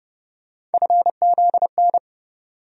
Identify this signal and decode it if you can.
Morse code, heard as a keyed tone, about 32 wpm, 700 Hz. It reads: FZN
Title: USA Call Sign Postfixes - 3 Letters 30wpm